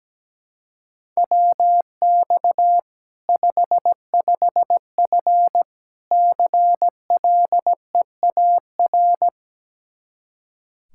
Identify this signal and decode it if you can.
Morse code, heard as a keyed tone, about 17 wpm, 700 Hz. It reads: WX 55F CLEAR
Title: Practice QSO - Exchange 2 - Info - 17wpm